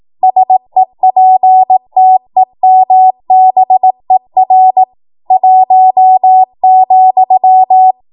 Morse code, heard as a keyed tone, about 18 wpm, 750 Hz. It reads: SEPTEMBER 1,
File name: Morse / 18WPM